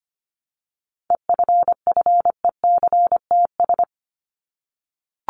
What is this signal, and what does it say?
Morse code, heard as a keyed tone, about 25 wpm, 700 Hz. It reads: EFFECTS
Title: ES:S Suffix - Top 100 Words with Base Words 25wpm